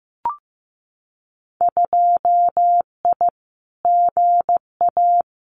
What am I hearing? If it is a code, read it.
Morse code, heard as a keyed tone, about 15 wpm, 700 Hz. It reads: 2I GA
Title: Winter Field Day Exchanges 15wpm